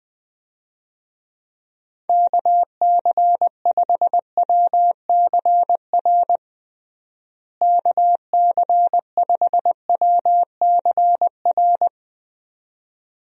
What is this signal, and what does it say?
Morse code, heard as a keyed tone, about 20 wpm, 700 Hz. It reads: KC5WCR KC5WCR
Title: USA Call Signs - 6 Characters - Repeated 2x - 20wpm